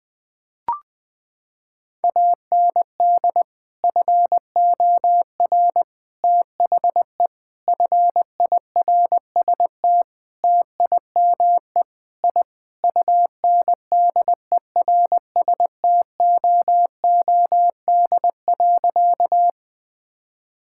Morse code, heard as a keyed tone, about 21 wpm, 700 Hz. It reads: AND FOR THE FIRST TIME I UNDERSTOOD.
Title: Sentences with 5-Word Phrases - 20wpm